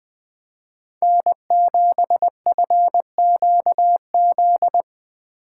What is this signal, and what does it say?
Morse code, heard as a keyed tone, about 21 wpm, 700 Hz. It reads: N7FQZ
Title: USA Call Signs - 2x Character Spacing - 20wpm